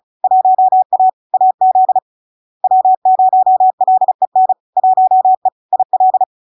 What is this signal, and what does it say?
Morse code, heard as a keyed tone, about 36 wpm, 750 Hz. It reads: 1A AZ W0LEN 1E IL
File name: LARK-FD-2022-Log-in-CW-35wpm-750-Hz